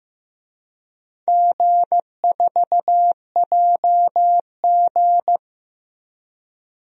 Morse code, heard as a keyed tone, about 15 wpm, 700 Hz. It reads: G4JG
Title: DXCC-100 Call Signs - 4 Characters - 15wpm